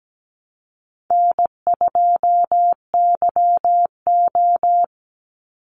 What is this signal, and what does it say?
Morse code, heard as a keyed tone, about 17 wpm, 700 Hz. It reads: N2YO
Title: Easy Call Signs Repeated 3x - 17wpm